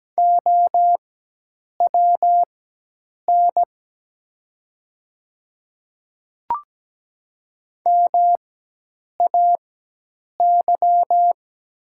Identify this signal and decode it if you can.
Morse code, heard as a keyed tone, about 17 wpm, spaced as for 8 wpm, 700 Hz. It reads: OWN MAY
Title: 4x Character Spacing - Top 300 Words - 17wpm